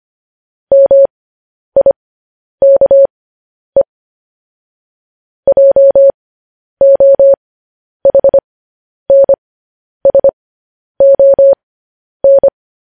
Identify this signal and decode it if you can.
Morse code, heard as a keyed tone, about 25 wpm, spaced as for 10 wpm, 550 Hz. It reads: MIKE JOHNSON